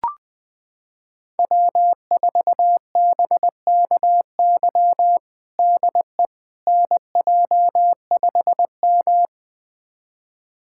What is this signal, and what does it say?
Morse code, heard as a keyed tone, about 20 wpm, 700 Hz. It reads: W4BKY DE NJ5M
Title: USA Callsign DE Callsign 20wpm